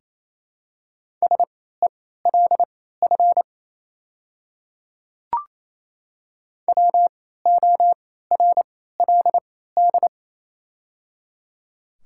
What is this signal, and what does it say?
Morse code, heard as a keyed tone, about 28 wpm, spaced as for 16 wpm, 700 Hz. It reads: SELF WORLD
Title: 3x Character Spacing - Top 200 Words - 28wpm